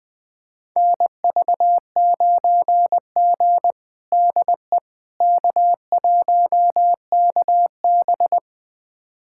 Morse code, heard as a keyed tone, about 21 wpm, 700 Hz. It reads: NV9G DE K1KB
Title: USA Callsign DE Callsign - 4 Characters - 20wpm